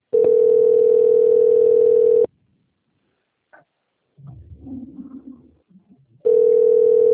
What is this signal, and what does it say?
Ringback tone (North America), 440 + 480 Hz.